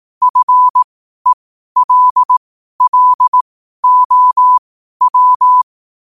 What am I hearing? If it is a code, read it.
Morse code, heard as a keyed tone, about 18 wpm, spaced as for 12 wpm, 1000 Hz. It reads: FELLOW
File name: qotdfiles / 12wpm